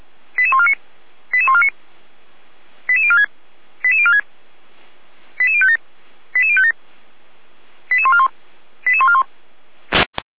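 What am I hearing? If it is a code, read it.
Five-tone selective call (ZVEI), heard as quick runs of five tones, tones of about 70 ms. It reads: 88149 88149 88046 88046 88057 88057 88141 88141
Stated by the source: ZVEI five-tone selective call 88150 received on an SDR.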